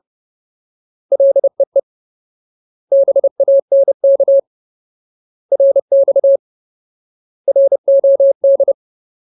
Morse code, heard as a keyed tone, about 30 wpm, 550 Hz. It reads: LEE BANK RX ROD
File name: Ultimate Word List / 30WPM Ultimate